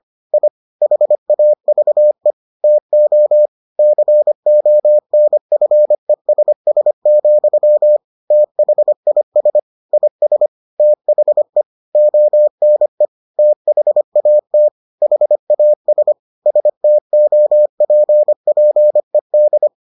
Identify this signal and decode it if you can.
Morse code, heard as a keyed tone, about 25 wpm, 600 Hz. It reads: I HAVE TO CONFESS, THIS IS THE ONE THAT HAS STOPPED